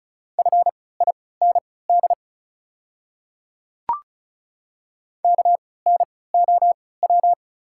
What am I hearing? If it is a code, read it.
Morse code, heard as a keyed tone, about 36 wpm, spaced as for 20 wpm, 690 Hz. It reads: FIND KNOW
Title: 3x Character Spacing - Top 200 Words - 35wpm